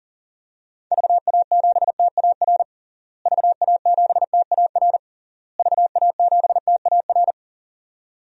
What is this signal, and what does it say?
Morse code, heard as a keyed tone, about 40 wpm, 700 Hz. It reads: VA7TAR VA7TAR VA7TAR
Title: DXCC-100 Call Signs - 6 Characters - Repeated 3x - 40wpm